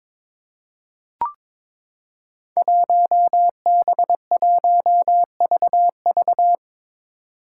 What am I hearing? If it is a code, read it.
Morse code, heard as a keyed tone, about 22 wpm, 700 Hz. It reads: 1B1VV
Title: Five Challenging Letter-Number Combinations 22wpm